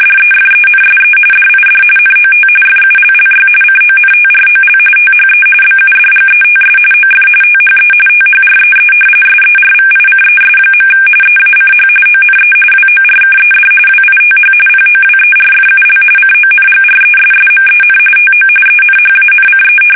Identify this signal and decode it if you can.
RTTY, about 45.45 baud. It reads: DEFENSE DID NOT FULLY CONTROL THE BUDGETS OF THE THREE SERVICES. EACH ONE WORKED WITH POWERFUL CONGRESSMEN TO ENHANCE
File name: fsk850 / usnavy5